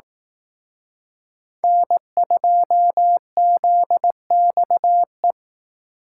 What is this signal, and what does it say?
Morse code, heard as a keyed tone, about 18 wpm, 700 Hz. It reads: N2ZXE
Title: Speed Racing - USA 1x3 Call Signs 15wpm